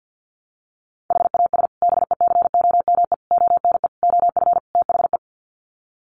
Morse code, heard as a keyed tone, about 50 wpm, 700 Hz. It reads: HAS BECOME ONE OF THE
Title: Sets of 5 Words Encore 50wpm